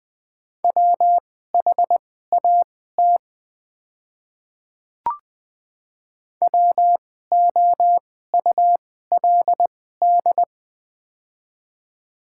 Morse code, heard as a keyed tone, about 20 wpm, spaced as for 14 wpm, 700 Hz. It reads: WHAT WOULD
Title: 2x Character Spacing - Top 100 Words - 20wpm